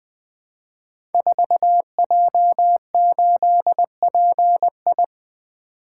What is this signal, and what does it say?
Morse code, heard as a keyed tone, about 20 wpm, 700 Hz. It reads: 4J8PI
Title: Code Groups - 5 Letter:Numbers - 20wpm